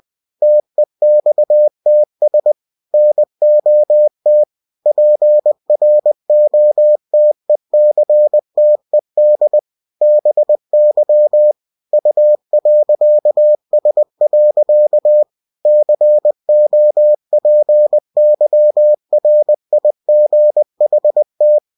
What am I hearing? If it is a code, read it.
Morse code, heard as a keyed tone, about 20 wpm, 600 Hz. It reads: TEXTS NOT PROTECTED BY U.S. COPYRIGHT
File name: cw-books / winnie-the-pooh / winnie-20wpm-page-0115